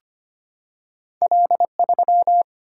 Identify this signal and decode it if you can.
Morse code, heard as a keyed tone, about 25 wpm, 700 Hz. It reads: L 3
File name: Two Letter-Number Combinations - 25wpm